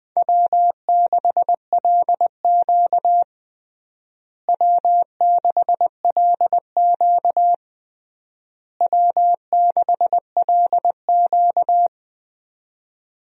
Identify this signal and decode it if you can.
Morse code, heard as a keyed tone, about 20 wpm, 700 Hz. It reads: W6LQ W6LQ W6LQ
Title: USA Call Signs - 4 Characters - Repeated 3x - 20wpm